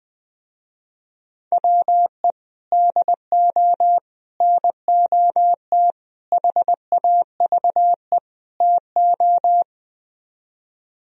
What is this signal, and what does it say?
Morse code, heard as a keyed tone, about 20 wpm, 700 Hz. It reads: WE DO NOT HAVE TO
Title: Speed-Racing - Sets of 5 Words 20wpm